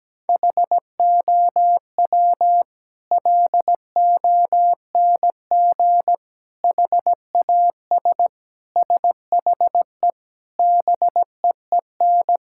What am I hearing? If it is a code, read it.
Morse code, heard as a keyed tone, about 17 wpm, 700 Hz. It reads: HOW LONG HAS SHE BEEN
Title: Sentences from Top 1000 Words - Review Part 6 - 17wpm